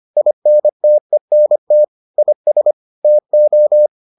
Morse code, heard as a keyed tone, about 25 wpm, 600 Hz. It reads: INTENT IS TO